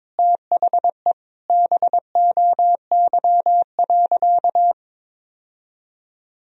Morse code, heard as a keyed tone, about 22 wpm, 700 Hz. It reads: THE BOY.
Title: Sentences from Top 1000 Words - Review Part 4 - 22wpm